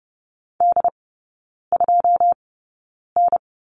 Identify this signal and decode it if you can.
Morse code, heard as a keyed tone, about 30 wpm, spaced as for 9 wpm, 700 Hz. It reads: D2N
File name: 081 Lesson 14e - Number 2 - Three Characters 30wpm